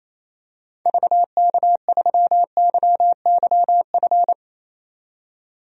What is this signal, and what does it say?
Morse code, heard as a keyed tone, about 27 wpm, 700 Hz. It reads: VK3YYF
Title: DXCC-100 Call Signs - 6 Characters - 28wpm